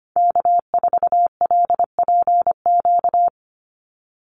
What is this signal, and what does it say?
Morse code, heard as a keyed tone, about 25 wpm, 700 Hz. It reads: K4LPQ
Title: NA SOTA Activator Callsigns 25wpm